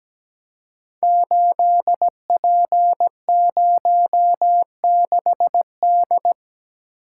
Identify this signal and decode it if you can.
Morse code, heard as a keyed tone, about 17 wpm, 700 Hz. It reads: 8P06D
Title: Code Groups - 5 Letter:Numbers - 17wpm